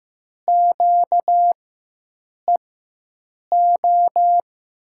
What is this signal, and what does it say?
Morse code, heard as a keyed tone, about 15 wpm, spaced as for 7 wpm, 700 Hz. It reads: QEO
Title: USA Call Signs - 6 Characters - 4x Character Spacing - 15wpm